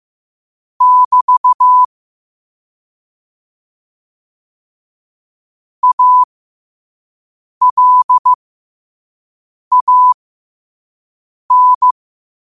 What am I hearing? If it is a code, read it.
Morse code, heard as a keyed tone, about 15 wpm, spaced as for 5 wpm, 1000 Hz. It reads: <BT> ALAN